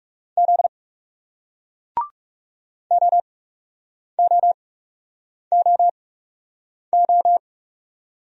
Morse code, heard as a keyed tone, about 40 wpm, spaced as for 8 wpm, 700 Hz. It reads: Z OOOO